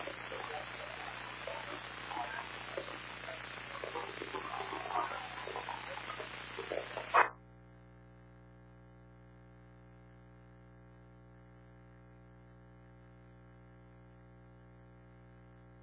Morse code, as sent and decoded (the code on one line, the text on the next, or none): none